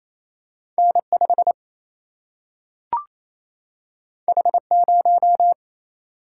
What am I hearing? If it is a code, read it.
Morse code, heard as a keyed tone, about 28 wpm, 700 Hz. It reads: N5 H0
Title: Single-Letter Single-Number 28wpm